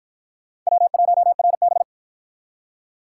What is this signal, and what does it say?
Morse code, heard as a keyed tone, about 44 wpm, 700 Hz. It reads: W1RB